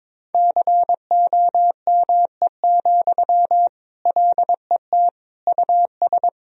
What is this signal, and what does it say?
Morse code, heard as a keyed tone, about 22 wpm, 700 Hz. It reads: COME, LET US
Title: Sentences from Top 1000 Words - Review Part 2 - 22wpm